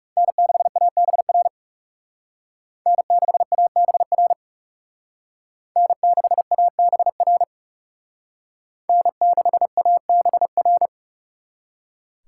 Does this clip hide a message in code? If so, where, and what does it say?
Yes: Morse code from 0.2 to 10.9 s, heard as a keyed tone, about 39 wpm, 700 Hz. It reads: N6ABR N6ABR N6ABR N6ABR